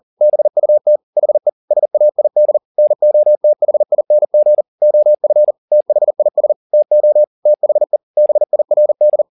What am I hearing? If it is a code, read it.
Morse code, heard as a keyed tone, about 40 wpm, 600 Hz. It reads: BUT HE SAID NOTHING OF THIS TO THE BIRD